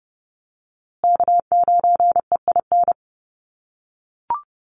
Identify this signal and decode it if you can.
Morse code, heard as a keyed tone, about 30 wpm, 700 Hz. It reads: K9EIN
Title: USA Call Signs Repeated 2x 30wpm